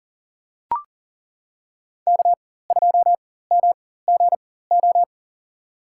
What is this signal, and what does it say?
Morse code, heard as a keyed tone, about 37 wpm, spaced as for 18 wpm, 700 Hz. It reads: K2MGO
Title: USA Call Signs - 5 Characters - 4x Character Spacing - 40wpm